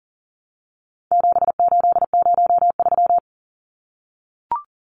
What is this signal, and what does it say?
Morse code, heard as a keyed tone, about 40 wpm, 700 Hz. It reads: 7803
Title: Numbers - 4-Digit 40wpm